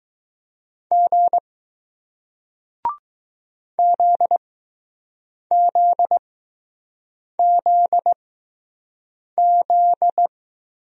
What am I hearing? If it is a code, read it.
Morse code, heard as a keyed tone, about 20 wpm, spaced as for 6 wpm, 700 Hz. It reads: G ZZZZ